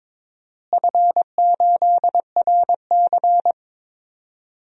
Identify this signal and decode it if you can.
Morse code, heard as a keyed tone, about 22 wpm, 700 Hz. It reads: F8RC
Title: DXCC-100 Call Signs - 4 Characters - Repeated 2x - 22wpm